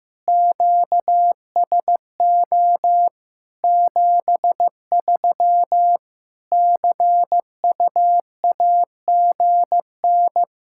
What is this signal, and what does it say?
Morse code, heard as a keyed tone, about 15 wpm, 700 Hz. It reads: QSO 73 CUAGN
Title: Practice QSO - Exchange 4 - Ending - 15wpm